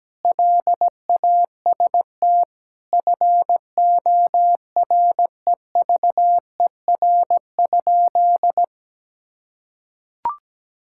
Morse code, heard as a keyed tone, about 17 wpm, 700 Hz. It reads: LAST FOREVER?
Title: Sentences from Top 1600 Words 17wpm